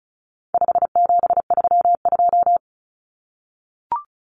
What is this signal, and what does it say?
Morse code, heard as a keyed tone, about 34 wpm, 700 Hz. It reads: 5732
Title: Numbers - 4-Digit 35wpm